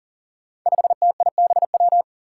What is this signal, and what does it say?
Morse code, heard as a keyed tone, about 40 wpm, 700 Hz. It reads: 5TIBW
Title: Five Challenging Letter-Number Combinations 40wpm